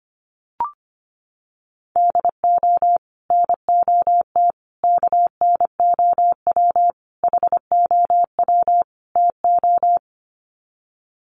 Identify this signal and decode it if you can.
Morse code, heard as a keyed tone, about 25 wpm, 700 Hz. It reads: DO NOT KNOW HOW TO
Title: Sets of 5 Words 25wpm